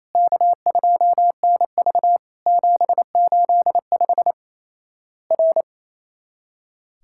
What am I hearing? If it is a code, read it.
Morse code, heard as a keyed tone, about 28 wpm, 700 Hz. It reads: K2NV 785 R
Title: Worked All Europe DX Contest - QTC Exchange - 28wpm